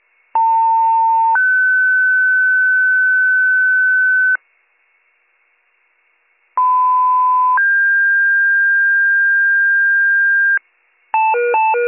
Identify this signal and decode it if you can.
Two-tone page, heard as a one-second tone, then a three-second tone, about 900 then 1500 Hz, about 1010 then 1600 Hz. It is heard from 0.3 to 10.6 s.